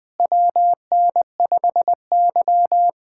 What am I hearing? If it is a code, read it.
Morse code, heard as a keyed tone, about 21 wpm, 700 Hz. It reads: WN5Y
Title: USA Callsign DE Callsign - 4 Characters - 20wpm